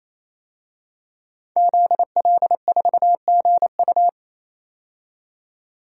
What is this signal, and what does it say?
Morse code, heard as a keyed tone, about 28 wpm, 700 Hz. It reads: ZL4GU
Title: DXCC-100 Call Signs - 5 Characters - 28wpm